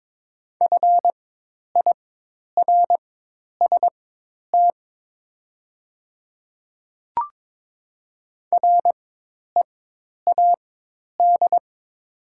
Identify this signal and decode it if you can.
Morse code, heard as a keyed tone, about 22 wpm, spaced as for 10 wpm, 690 Hz. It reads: FIRST READ
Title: 4x Character Spacing - Top 200 Words - 22wpm